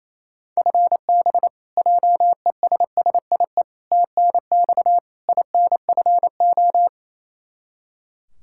Morse code, heard as a keyed tone, about 27 wpm, 700 Hz. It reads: FB JESSIE TNX INFO
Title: Practice QSO - Exchange 3 - More Info - 28wpm